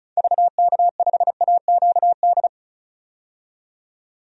Morse code, heard as a keyed tone, about 35 wpm, 700 Hz. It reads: VK5AQD
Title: DXCC-100 Call Signs - 6 Characters - 35wpm